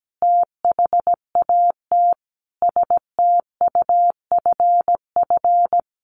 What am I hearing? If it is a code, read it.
Morse code, heard as a keyed tone, about 17 wpm, 700 Hz. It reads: THAT STUFF
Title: Sentences from Top 1400 Words 17wpm